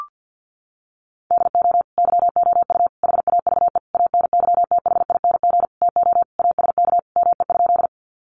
Morse code, heard as a keyed tone, about 50 wpm, 700 Hz. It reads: DO YOU HAVE ANYTHING TO ASK ME?
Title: Sentences from Top 1000 Words 50wpm